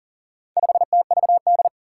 Morse code, heard as a keyed tone, about 37 wpm, 700 Hz. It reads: 5TVD
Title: Four Challenging Letter-Number Combinations 40wpm